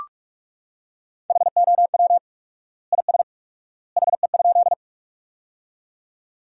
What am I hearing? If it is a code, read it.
Morse code, heard as a keyed tone, about 44 wpm, 700 Hz. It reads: HOW IS HE?
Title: 4x Word-Spacing - Sentences from Top 100 Words - 45wpm